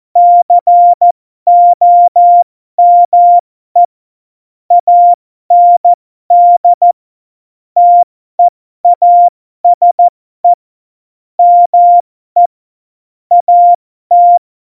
Morse code, heard as a keyed tone, about 14 wpm, spaced as for 12 wpm, 700 Hz. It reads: COME AND TEASE ME AT
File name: Morsekurs / Witze